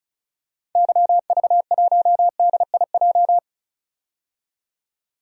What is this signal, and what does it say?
Morse code, heard as a keyed tone, about 34 wpm, 700 Hz. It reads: YV1DIJ
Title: DXCC-100 Call Signs - 6 Characters - 35wpm